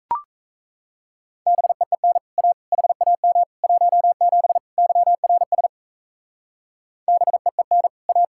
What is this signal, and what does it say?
Morse code, heard as a keyed tone, about 34 wpm, 700 Hz. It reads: BEEN A HAM 17 YRS BEEN A
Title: Speed-Racing Common QSO Phrases - 28wpm